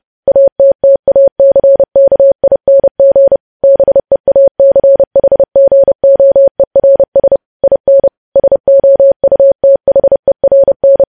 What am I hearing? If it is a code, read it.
Morse code, heard as a keyed tone, about 30 wpm, 550 Hz. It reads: ATTACKING BEACHGOERS IN SOUTHERN
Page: news headlines in morse code at 30 wpm